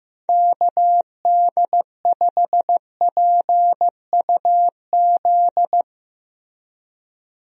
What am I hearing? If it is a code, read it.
Morse code, heard as a keyed tone, about 15 wpm, 700 Hz. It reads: KD5PUZ
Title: USA Call Signs - 6 Characters - Repeated 2x - 15wpm